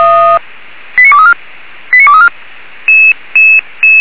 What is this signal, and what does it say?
Five-tone selective call (ZVEI), heard as quick runs of five tones, tones of about 80 ms. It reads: 89234 89234